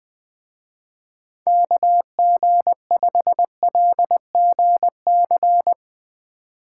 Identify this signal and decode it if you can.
Morse code, heard as a keyed tone, about 20 wpm, 700 Hz. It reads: KG5LGC
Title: Speed Racing - USA 2x3 Call Signs 17wpm